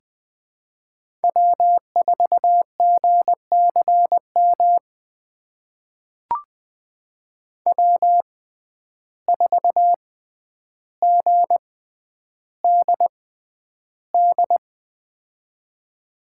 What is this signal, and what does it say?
Morse code, heard as a keyed tone, about 20 wpm, spaced as for 10 wpm, 700 Hz. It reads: W4GCM W4GDD
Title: USA Call Signs - 5 Characters - 6x Character Spacing - 20wpm